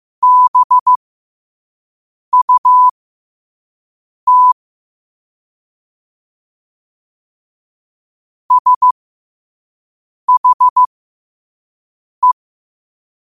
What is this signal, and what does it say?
Morse code, heard as a keyed tone, about 15 wpm, spaced as for 5 wpm, 1000 Hz. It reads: BUT SHE
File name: qotdfiles / 5wpm